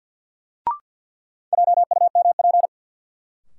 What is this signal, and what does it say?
Morse code, heard as a keyed tone, about 52 wpm, 700 Hz. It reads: JUMP
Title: Sentences from Top 1300 Words - New Word Drill Alt - 50wpm